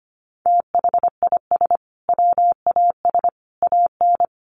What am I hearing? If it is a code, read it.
Morse code, heard as a keyed tone, about 25 wpm, 700 Hz. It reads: THIS WAS AN
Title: Sentences from Top 1300 Words 25wpm